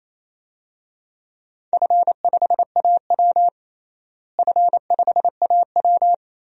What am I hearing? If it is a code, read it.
Morse code, heard as a keyed tone, about 28 wpm, 700 Hz. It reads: F5AW F5AW
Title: DXCC-100 Call Signs - 4 Characters - Repeated 2x - 28wpm